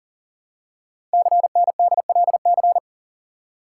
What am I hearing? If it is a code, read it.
Morse code, heard as a keyed tone, about 40 wpm, 700 Hz. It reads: CNDLC
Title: Code Groups - 5 Letters - 40wpm